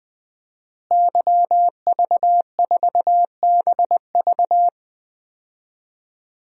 Morse code, heard as a keyed tone, about 20 wpm, 700 Hz. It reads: YV4BV
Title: DXCC-100 Call Signs - 5 Characters - Repeated 3x - 20wpm